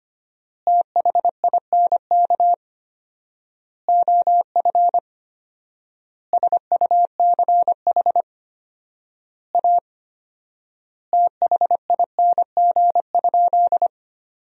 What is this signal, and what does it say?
Morse code, heard as a keyed tone, about 25 wpm, 700 Hz. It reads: THINK OF SUCH A THING?
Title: 4x Word-Spacing - Sentences from Top 200 Words - 25wpm